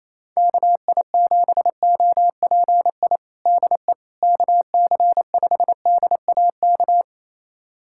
Morse code, heard as a keyed tone, about 28 wpm, 700 Hz. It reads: KI7OPI DE KC5DAK
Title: USA Callsign DE Callsign 28wpm